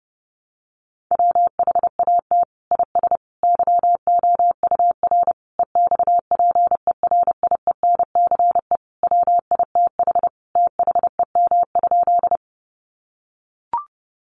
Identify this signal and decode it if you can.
Morse code, heard as a keyed tone, about 30 wpm, 700 Hz. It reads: WHAT IS YOUR EXPERIENCE WITH THEM?
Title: Sentences from Top 600 Words 30wpm